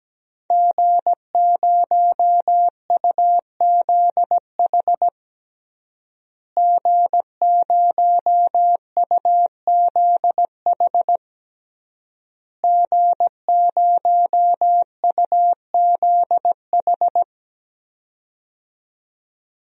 Morse code, heard as a keyed tone, about 17 wpm, 700 Hz. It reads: G0UZH G0UZH G0UZH
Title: DXCC-100 Call Signs - 5 Characters - Repeated 3x - 17wpm